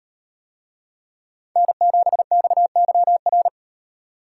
Morse code, heard as a keyed tone, about 38 wpm, 700 Hz. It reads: N7XYR